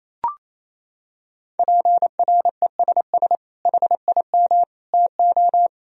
Morse code, heard as a keyed tone, about 28 wpm, 700 Hz. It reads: PRESS HIM TO
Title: Sentences from Top 1000 Words - Review Part 1 - 28wpm